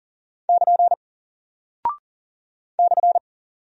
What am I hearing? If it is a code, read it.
Morse code, heard as a keyed tone, about 40 wpm, 700 Hz. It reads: <KN> /
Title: Single Character (Letter:Number:Punctuation:Prosign) Alt 40wpm